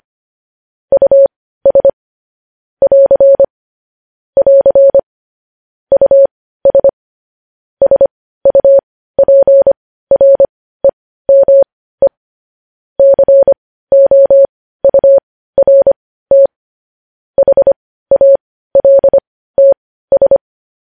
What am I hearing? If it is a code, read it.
Morse code, heard as a keyed tone, about 25 wpm, spaced as for 15 wpm, 550 Hz. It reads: US <AR> <AR> US SUPREME COURT HALTS
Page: news headlines in morse code at 15 wpm